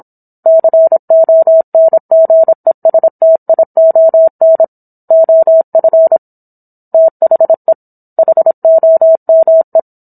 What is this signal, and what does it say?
Morse code, heard as a keyed tone, about 27 wpm, 650 Hz. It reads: CONGESTION OF THE HOME